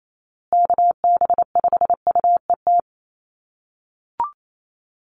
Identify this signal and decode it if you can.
Morse code, heard as a keyed tone, about 28 wpm, 700 Hz. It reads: KB5UET
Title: USA Call Signs Repeated 3x 28wpm